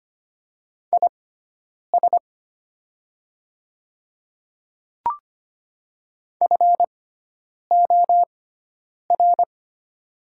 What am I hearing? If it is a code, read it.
Morse code, heard as a keyed tone, about 25 wpm, spaced as for 9 wpm, 700 Hz. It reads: IS FOR
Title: 6x Character Spacing - Top 200 Words - 25wpm